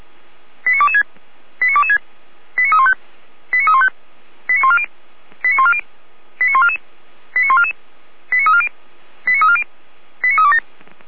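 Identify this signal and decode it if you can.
Five-tone selective call (ZVEI), heard as quick runs of five tones, tones of about 70 ms. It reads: 89226 89226 89315 89315 89140 89140 89144 89144 89340 89340 89327
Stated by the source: ZVEI five-tone call 89356 received by SDR.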